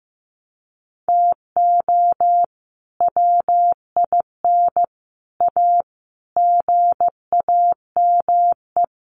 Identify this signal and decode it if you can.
Morse code, heard as a keyed tone, about 15 wpm, 700 Hz. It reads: TO WIN A GAME
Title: Sentences from Top 1500 Words 15wpm